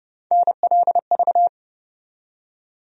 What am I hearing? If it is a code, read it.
Morse code, heard as a keyed tone, about 30 wpm, 700 Hz. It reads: NLV
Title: USA Call Sign Postfixes - 3 Letters 30wpm